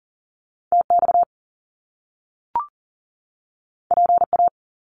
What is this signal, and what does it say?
Morse code, heard as a keyed tone, about 42 wpm, 700 Hz. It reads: TX PA
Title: US States 40wpm